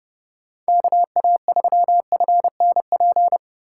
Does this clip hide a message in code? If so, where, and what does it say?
Yes: Morse code from 0.7 to 3.4 s, heard as a keyed tone, about 35 wpm, 700 Hz. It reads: KA3FNP